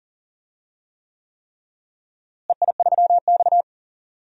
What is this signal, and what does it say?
Morse code, heard as a keyed tone, about 40 wpm, 700 Hz. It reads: EI3X
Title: DXCC-100 Call Signs - 4 Characters - Repeated 3x - 40wpm